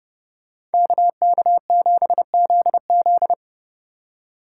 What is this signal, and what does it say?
Morse code, heard as a keyed tone, about 29 wpm, 700 Hz. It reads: KK7ZZ